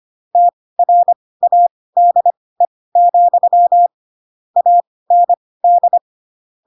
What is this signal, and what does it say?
Morse code, heard as a keyed tone, about 25 wpm, spaced as for 18 wpm, 700 Hz. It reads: TRADE, AND